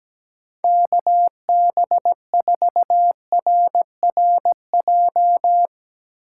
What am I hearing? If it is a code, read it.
Morse code, heard as a keyed tone, about 17 wpm, 700 Hz. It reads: KB4RRJ
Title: Speed Racing - USA 2x3 Call Signs 17wpm